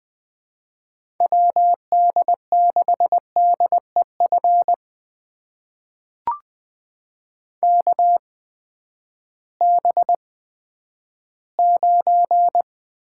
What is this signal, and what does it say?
Morse code, heard as a keyed tone, about 20 wpm, spaced as for 11 wpm, 700 Hz. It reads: WD6DEF KB9
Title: USA Call Signs - 6 Characters - 8x Character Spacing - 20wpm